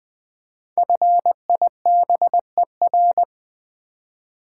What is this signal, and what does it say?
Morse code, heard as a keyed tone, about 20 wpm, 700 Hz. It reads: FIBER
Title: 5 Letter Words - First 3 Letters Given - 20wpm